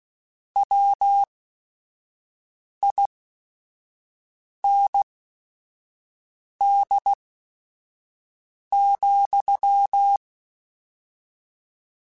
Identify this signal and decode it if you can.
Morse code, heard as a keyed tone, about 16 wpm, spaced as for 5 wpm, 800 Hz. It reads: WIND,
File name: Weather Sets / Weather Sets-mp3s / Weather 3 17 @5